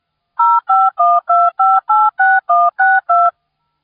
DTMF, heard as telephone touch tones, tones of about 220 ms. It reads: 0512586162